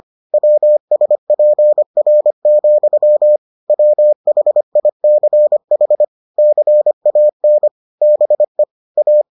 Morse code, heard as a keyed tone, about 25 wpm, 600 Hz. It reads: WSPR, WHICH CAN BE A